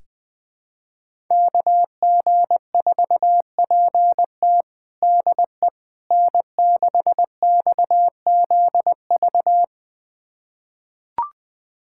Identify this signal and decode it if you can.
Morse code, heard as a keyed tone, about 21 wpm, 700 Hz. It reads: KG4PT DE N6XZV
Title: USA Callsign DE Callsign - 5 Characters - 20wpm